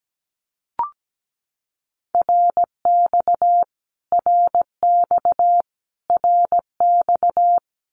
Morse code, heard as a keyed tone, about 17 wpm, 700 Hz. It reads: RX RX RX
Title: Abbreviations Repeated 3x - 17wpm